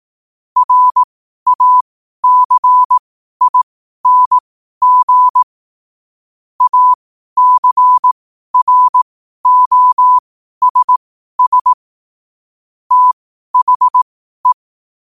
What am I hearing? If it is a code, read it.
Morse code, heard as a keyed tone, about 18 wpm, spaced as for 12 wpm, 1000 Hz. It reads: RACING ACROSS THE
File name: qotdfiles / 12wpm